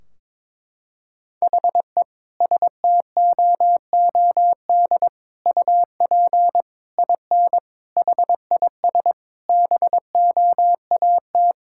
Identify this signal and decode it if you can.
Morse code, heard as a keyed tone, about 22 wpm, 700 Hz. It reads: HE STOOD UP IN HIS BOAT
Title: Sentences from Top 1600 Words 22wpm